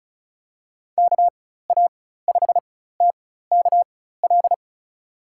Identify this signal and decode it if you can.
Morse code, heard as a keyed tone, about 34 wpm, spaced as for 16 wpm, 700 Hz. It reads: KA5TKL
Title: USA Call Signs - 6 Characters - 4x Character Spacing - 35wpm